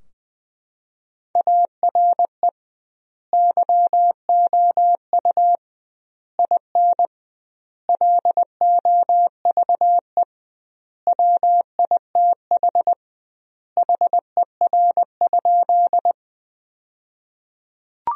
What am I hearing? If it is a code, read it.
Morse code, heard as a keyed tone, about 20 wpm, 700 Hz. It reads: ARE YOU IN LOVE WITH HER?
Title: 2x Word-Spacing - Sentences from Top 400 Words - 20wpm